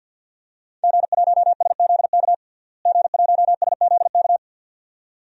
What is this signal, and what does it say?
Morse code, heard as a keyed tone, about 47 wpm, 700 Hz. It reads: G1SZK G1SZK
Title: DXCC-100 Call Signs - 5 Characters - Repeated 2x - 50wpm